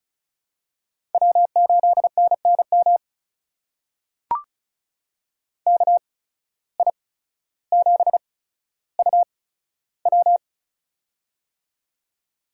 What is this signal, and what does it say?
Morse code, heard as a keyed tone, about 35 wpm, spaced as for 15 wpm, 700 Hz. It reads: W8NNM KI7UW
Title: USA Call Signs - 5 Characters - 8x Character Spacing - 35wpm